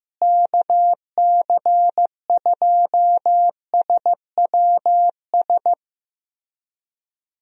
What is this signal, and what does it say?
Morse code, heard as a keyed tone, about 15 wpm, 700 Hz. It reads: KC2SWS
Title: USA Call Signs - 6 Characters - Repeated 3x - 15wpm